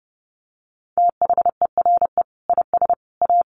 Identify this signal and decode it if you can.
Morse code, heard as a keyed tone, about 30 wpm, 700 Hz. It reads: THERE IS A
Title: Sentences from Top 1100 Words 30wpm